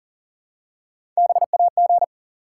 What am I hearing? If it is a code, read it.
Morse code, heard as a keyed tone, about 40 wpm, 700 Hz. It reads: BAG
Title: Three Letter Words - 40wpm